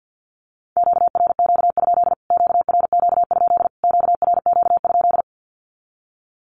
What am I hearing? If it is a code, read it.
Morse code, heard as a keyed tone, about 50 wpm, 700 Hz. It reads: QRQ? QRQ? QRQ?
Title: Q-Codes Repeated 3x 50wpm